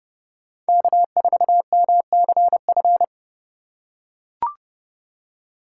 Morse code, heard as a keyed tone, about 32 wpm, 700 Hz. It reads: K4MCF
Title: USA Call Signs - 5 Characters - 4x Character Spacing - 30wpm